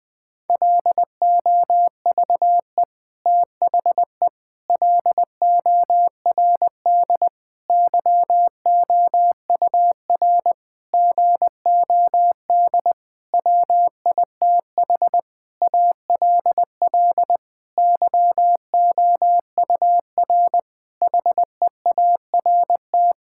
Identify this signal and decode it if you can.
Morse code, heard as a keyed tone, about 20 wpm, 700 Hz. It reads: LOVE THE LORD YOUR GOD WITH ALL YOUR HEART
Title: Sentences with 9-Word Phrases - Spoken Beginning - 20wpm